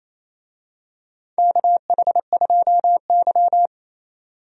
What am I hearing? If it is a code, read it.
Morse code, heard as a keyed tone, about 28 wpm, 700 Hz. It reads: KH2Y
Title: USA Call Signs - 4 Characters - 28wpm